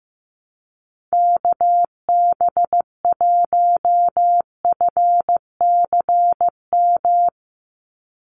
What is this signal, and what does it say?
Morse code, heard as a keyed tone, about 15 wpm, 700 Hz. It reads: KB1FCM
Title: USA Call Signs 15wpm